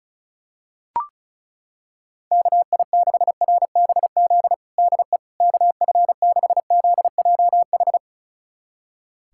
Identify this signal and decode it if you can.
Morse code, heard as a keyed tone, about 34 wpm, 700 Hz. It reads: KI6RBZ DE KF6ZJH
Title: USA Callsign DE Callsign - 6 Characters - 35wpm